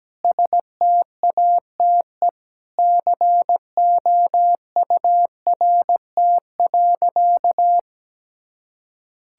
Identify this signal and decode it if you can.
Morse code, heard as a keyed tone, about 17 wpm, 700 Hz. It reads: STATE COURT.
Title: Sentences from Top 1600 Words 17wpm